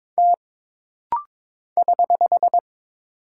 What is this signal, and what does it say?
Morse code, heard as a keyed tone, about 22 wpm, 700 Hz. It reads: T <HH>
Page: This rapid-fire timing is geared to push your ICR (Instant Character Recognition) skills.